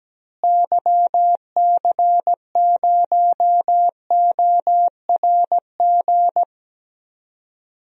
Morse code, heard as a keyed tone, about 17 wpm, 700 Hz. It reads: YC0ORG
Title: DXCC-100 Call Signs - 6 Characters - Repeated 2x - 17wpm